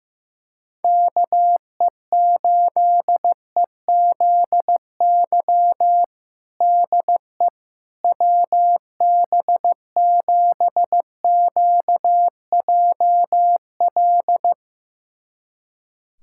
Morse code, heard as a keyed tone, about 15 wpm, 700 Hz. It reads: KE8EZY DE WB7QJL
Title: USA Callsign DE Callsign - 6 Characters - 15wpm